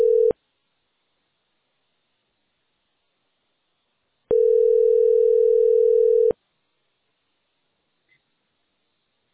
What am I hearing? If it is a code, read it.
Ringback tone (North America), 440 + 480 Hz.